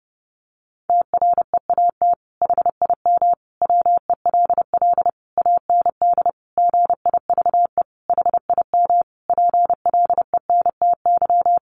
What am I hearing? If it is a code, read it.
Morse code, heard as a keyed tone, about 30 wpm, 700 Hz. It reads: TREAT HIM WELL AND GIVE HIM PLENTY
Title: Sentences from Top 1300 Words 30wpm